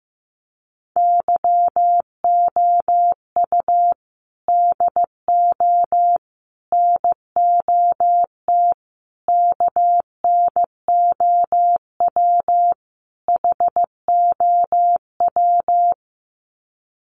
Morse code, heard as a keyed tone, about 15 wpm, 700 Hz. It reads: YOU DO NOT KNOW HOW
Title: Sets of 5 Words Encore 15wpm